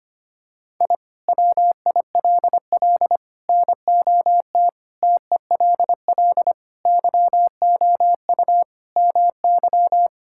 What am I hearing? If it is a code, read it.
Morse code, heard as a keyed tone, about 25 wpm, 700 Hz. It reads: I WILL NOT TELL YOU MY
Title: Sentences from Top 1000 Words - Review Part 5 - 25wpm